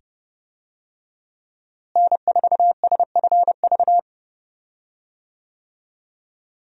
Morse code, heard as a keyed tone, about 31 wpm, 700 Hz. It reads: N4SFV
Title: USA Call Signs - 5 Characters - Repeated 2x - 30wpm